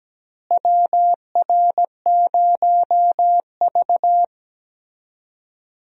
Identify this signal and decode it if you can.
Morse code, heard as a keyed tone, about 17 wpm, 700 Hz. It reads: WR0V
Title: USA Call Signs - 4 Characters - Repeated 2x - 17wpm